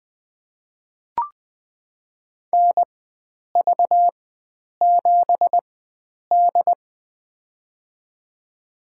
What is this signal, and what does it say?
Morse code, heard as a keyed tone, about 20 wpm, spaced as for 9 wpm, 700 Hz. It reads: NV7D
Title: USA Call Signs - 4 Characters - 4x Character Spacing - 20wpm